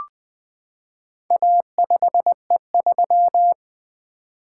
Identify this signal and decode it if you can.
Morse code, heard as a keyed tone, about 20 wpm, 700 Hz. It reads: A5E3
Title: Four Letter-Number Combinations - 20wpm